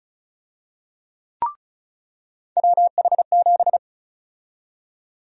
Morse code, heard as a keyed tone, about 36 wpm, 700 Hz. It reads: WH7
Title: USA Call Sign Prefixes - 35wpm